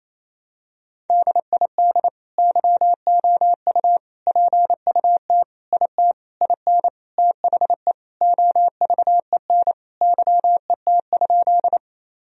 Morse code, heard as a keyed tone, about 28 wpm, 700 Hz. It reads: DID YOU PUT IT IN THE OVEN YET?
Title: Sentences from Top 1000 Words - Review Part 4 - 28wpm